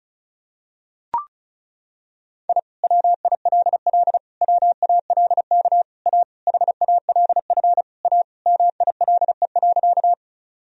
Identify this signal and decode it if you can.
Morse code, heard as a keyed tone, about 36 wpm, 700 Hz. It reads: I WILL WALK A HALF A MILE.
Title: Sentences from Top 1000 Words - Review Part 1 - 35wpm